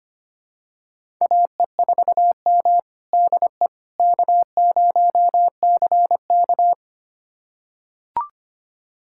Morse code, heard as a keyed tone, about 25 wpm, 700 Hz. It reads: AE4M DE K0CK
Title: USA Callsign DE Callsign - 4 Characters - 25wpm